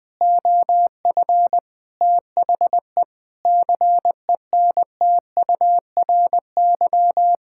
Morse code, heard as a keyed tone, about 20 wpm, 700 Hz. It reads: OF THE CENTURY
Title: Sentences with 5-Word Phrases - 20wpm